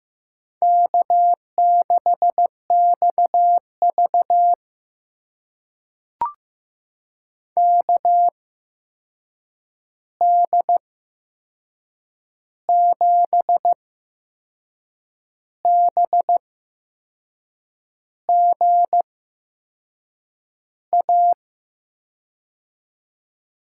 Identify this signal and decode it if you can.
Morse code, heard as a keyed tone, about 15 wpm, spaced as for 6 wpm, 700 Hz. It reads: K6XV KD7BGA
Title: USA Call Signs - 8x Character Spacing - 15wpm